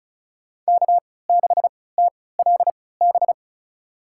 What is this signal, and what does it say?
Morse code, heard as a keyed tone, about 36 wpm, spaced as for 20 wpm, 700 Hz. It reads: K6TLB
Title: USA Call Signs - 5 Characters - 3x Character Spacing - 35wpm